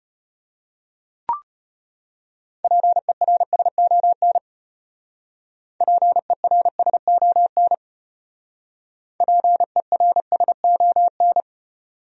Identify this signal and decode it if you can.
Morse code, heard as a keyed tone, about 32 wpm, 700 Hz. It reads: PERSON PERSON PERSON